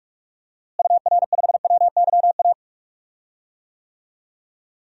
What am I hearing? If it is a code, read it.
Morse code, heard as a keyed tone, about 43 wpm, 700 Hz. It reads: UR5WYA
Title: DXCC-100 Call Signs - 6 Characters - Repeated 2x - 45wpm